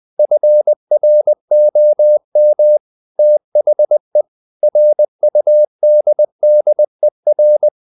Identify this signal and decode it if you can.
Morse code, heard as a keyed tone, about 20 wpm, 600 Hz. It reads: FROM THE RUDDER
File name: the old man and the sea part 2 20wpm